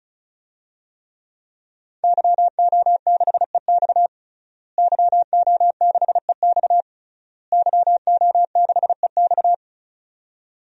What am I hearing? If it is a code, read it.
Morse code, heard as a keyed tone, about 36 wpm, 700 Hz. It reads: YO6EX YO6EX YO6EX
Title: DXCC-100 Call Signs - 5 Characters - Repeated 3x - 35wpm